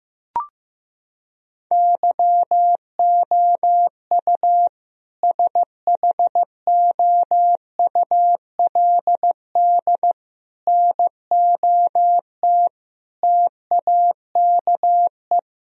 Morse code, heard as a keyed tone, about 16 wpm, 700 Hz. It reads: YOU SHOULD NOT TAKE
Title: Sentences from Top 1000 Words - Review Part 3 - 15wpm